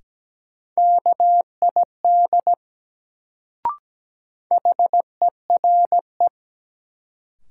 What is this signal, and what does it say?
Morse code, heard as a keyed tone, about 17 wpm, 700 Hz. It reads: KID HERE
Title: Sets of 5 Words - New Word Drill - 17wpm